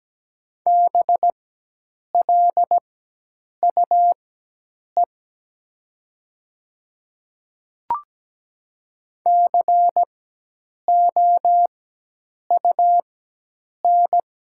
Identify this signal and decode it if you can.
Morse code, heard as a keyed tone, about 17 wpm, spaced as for 8 wpm, 700 Hz. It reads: BLUE COUN
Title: 4x Character Spacing - Top 500 Words - 17wpm